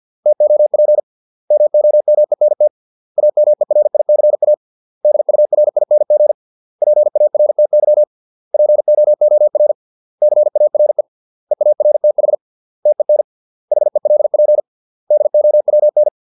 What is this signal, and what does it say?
Morse code, heard as a keyed tone, about 50 wpm, 600 Hz. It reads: TOP MOMENT AMERICA DURING PARTY POOR CARE EARTH TEN HELP DOWN